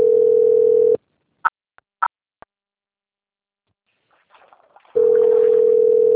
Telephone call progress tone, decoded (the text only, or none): ringback tone